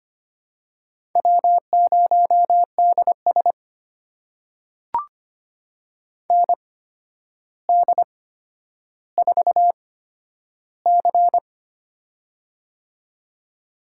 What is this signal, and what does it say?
Morse code, heard as a keyed tone, about 25 wpm, spaced as for 11 wpm, 700 Hz. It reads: W0DS ND4C
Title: USA Call Signs - 4 Characters - 8x Character Spacing - 25wpm